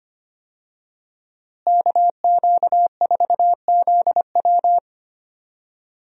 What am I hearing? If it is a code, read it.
Morse code, heard as a keyed tone, about 25 wpm, 700 Hz. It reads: KQ4ZW
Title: USA Call Signs - 5 Characters - Repeated 2x - 25wpm